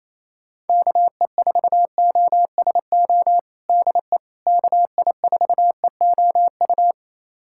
Morse code, heard as a keyed tone, about 28 wpm, 700 Hz. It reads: KE4OSO DE KI4EOU
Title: USA Callsign DE Callsign - 6 Characters - 28wpm